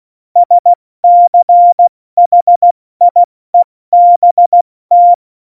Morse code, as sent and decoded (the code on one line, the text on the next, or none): ... -.-. .... .. -... -
SCHIEBT